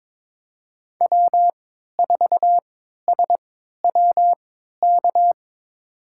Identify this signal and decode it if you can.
Morse code, heard as a keyed tone, about 22 wpm, spaced as for 13 wpm, 700 Hz. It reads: W4SWK
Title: USA Call Signs - 5 Characters - 3x Character Spacing - 22wpm